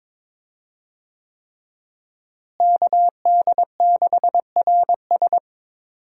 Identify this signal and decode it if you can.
Morse code, heard as a keyed tone, about 22 wpm, 700 Hz. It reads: KD6RS
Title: USA Call Signs - 5 Characters - Repeated 3x - 22wpm